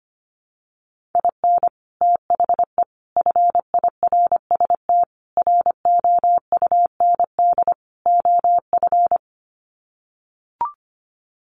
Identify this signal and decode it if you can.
Morse code, heard as a keyed tone, about 25 wpm, 700 Hz. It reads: IN THE FIRST ROUND OF
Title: Sets of 5 Words Encore 25wpm